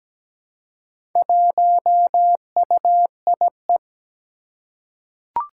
Morse code, heard as a keyed tone, about 17 wpm, 700 Hz. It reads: 1UIE
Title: Four Challenging Letter-Number Combinations 17wpm